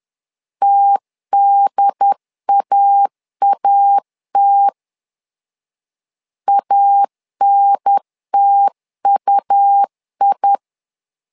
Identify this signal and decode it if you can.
Morse code, heard as a keyed tone, about 10 wpm, 800 Hz. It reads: TDAAT ANTUI
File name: LEZIONI / 2 UHDN / 2 UHDN 10wpm 1904